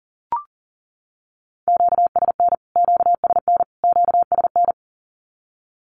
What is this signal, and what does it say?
Morse code, heard as a keyed tone, about 40 wpm, 700 Hz. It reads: QSN QSN QSN
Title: Q-Codes Repeated 3x 40wpm